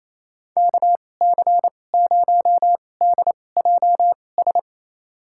Morse code, heard as a keyed tone, about 27 wpm, spaced as for 20 wpm, 700 Hz. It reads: KC0DJS
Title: USA Call Signs - 6 Characters - 2x Character Spacing - 28wpm